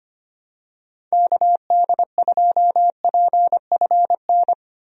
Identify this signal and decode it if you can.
Morse code, heard as a keyed tone, about 25 wpm, 700 Hz. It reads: KD2PFN
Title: USA Call Signs - 6 Characters - 25wpm